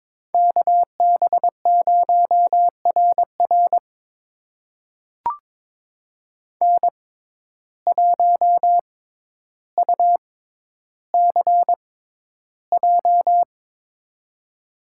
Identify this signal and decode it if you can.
Morse code, heard as a keyed tone, about 22 wpm, spaced as for 11 wpm, 700 Hz. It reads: KB0RR N1UCJ
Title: USA Call Signs - 5 Characters - 6x Character Spacing - 22wpm